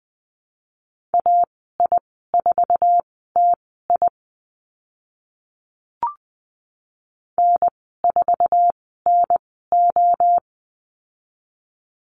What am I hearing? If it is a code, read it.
Morse code, heard as a keyed tone, about 20 wpm, spaced as for 15 wpm, 700 Hz. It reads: AI4TI N4NO
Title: 048 Lesson 9g - Number 4 - Call Signs 20wpm